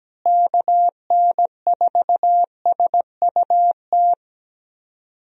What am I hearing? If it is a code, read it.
Morse code, heard as a keyed tone, about 17 wpm, 700 Hz. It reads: KN4SUT
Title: USA Call Signs - 6 Characters - Repeated 2x - 17wpm